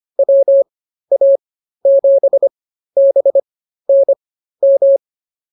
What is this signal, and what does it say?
Morse code, heard as a keyed tone, about 24 wpm, spaced as for 13 wpm, 550 Hz. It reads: WA7BNM
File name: QSO106 13